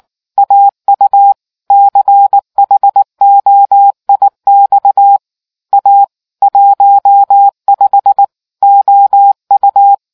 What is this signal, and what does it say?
Morse code, heard as a keyed tone, about 19 wpm, 800 Hz. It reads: AU CHOIX A 15 OU